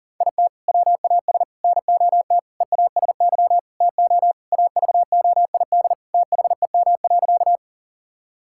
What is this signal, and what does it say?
Morse code, heard as a keyed tone, about 37 wpm, 700 Hz. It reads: IT WAS NOT EASY TO AVOID THEM.
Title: Sentences from Top 1000 Words - Review Part 6 - 40wpm